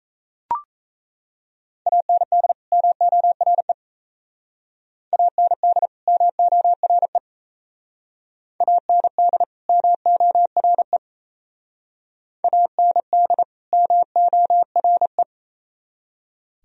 Morse code, heard as a keyed tone, about 36 wpm, 700 Hz. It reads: AND MORE AND MORE AND MORE AND MORE